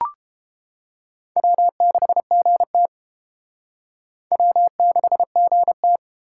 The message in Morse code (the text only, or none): W6GT W6GT